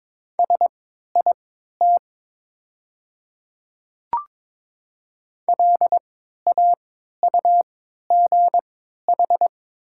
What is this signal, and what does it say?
Morse code, heard as a keyed tone, about 23 wpm, spaced as for 13 wpm, 700 Hz. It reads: SIT LAUGH
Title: 3x Character Spacing - Top 500 Words - 22wpm